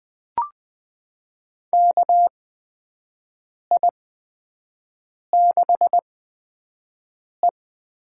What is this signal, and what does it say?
Morse code, heard as a keyed tone, about 21 wpm, spaced as for 5 wpm, 700 Hz. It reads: KI6E
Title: USA Call Signs - 8x Character Spacing - 20wpm